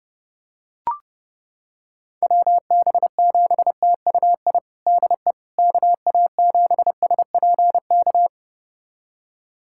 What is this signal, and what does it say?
Morse code, heard as a keyed tone, about 32 wpm, 700 Hz. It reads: WB7TUI DE KA7SPK
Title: USA Callsign DE Callsign 30wpm